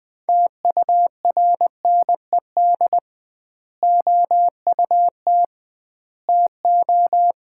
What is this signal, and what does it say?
Morse code, heard as a keyed tone, about 21 wpm, 700 Hz. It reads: TURNED OUT TO
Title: 2x Word-Spacing - Sets of 4 Words - 20wpm